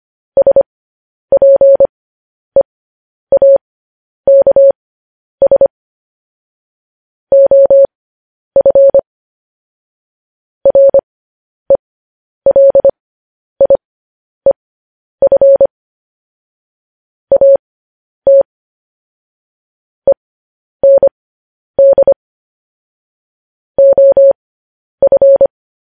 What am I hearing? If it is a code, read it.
Morse code, heard as a keyed tone, about 25 wpm, spaced as for 10 wpm, 550 Hz. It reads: SPEAKS OF RELIEF AT END OF
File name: podcast / News headlines-1x-25x10